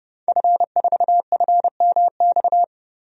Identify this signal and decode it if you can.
Morse code, heard as a keyed tone, about 32 wpm, 700 Hz. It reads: F4FMX
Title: DXCC-100 Call Signs - 5 Characters - 30wpm